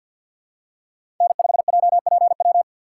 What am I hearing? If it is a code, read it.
Morse code, heard as a keyed tone, about 53 wpm, 700 Hz. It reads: N5JPW
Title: USA Call Signs - 5 Characters - 6x Character Spacing - 50wpm